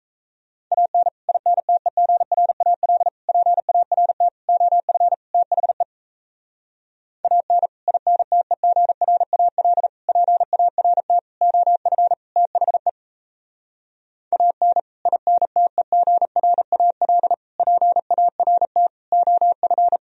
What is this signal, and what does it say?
Morse code, heard as a keyed tone, about 37 wpm, 700 Hz. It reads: AN INTEGRAL PART OF THE AN INTEGRAL PART OF THE AN INTEGRAL PART OF